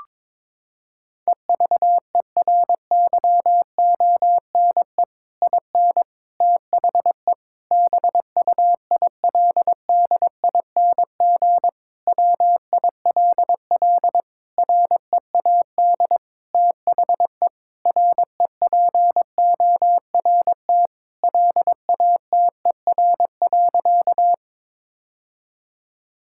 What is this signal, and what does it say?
Morse code, heard as a keyed tone, about 22 wpm, 700 Hz. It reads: EVERYONE IN THE BUILDING WILL READ THE REPORT LATER.
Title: Sentences from Top 1000 Words - Review Part 6 - 22wpm